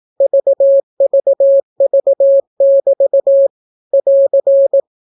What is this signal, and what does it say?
Morse code, heard as a keyed tone, about 18 wpm, 550 Hz. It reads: VVV<BT> <AR>
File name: naukapodstaw / skroty / skroty